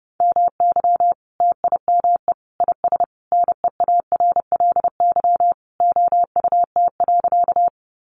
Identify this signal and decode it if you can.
Morse code, heard as a keyed tone, about 30 wpm, 700 Hz. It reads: MY TIME IS NEARLY OUT.
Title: Sentences from Top 1300 Words 30wpm